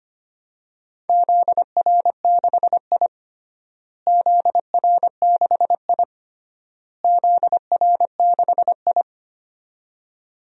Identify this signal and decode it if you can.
Morse code, heard as a keyed tone, about 25 wpm, 700 Hz. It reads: ZR6I ZR6I ZR6I
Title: DXCC-100 Call Signs - 4 Characters - Repeated 3x - 25wpm